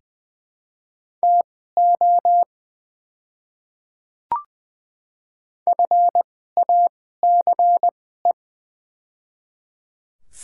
Morse code, heard as a keyed tone, about 21 wpm, spaced as for 15 wpm, 700 Hz. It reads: TO FACE
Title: 2x Character Spacing - Top 300 Words - 20wpm